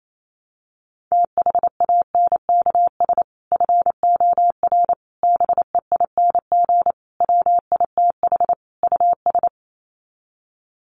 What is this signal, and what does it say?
Morse code, heard as a keyed tone, about 28 wpm, 700 Hz. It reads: THANKS FOR BEING WITH US
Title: Sets of 5 Words 28wpm